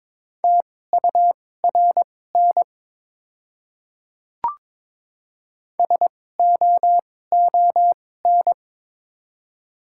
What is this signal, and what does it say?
Morse code, heard as a keyed tone, about 22 wpm, spaced as for 16 wpm, 700 Hz. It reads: TURN SOON
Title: 2x Character Spacing - Top 500 Words - 22wpm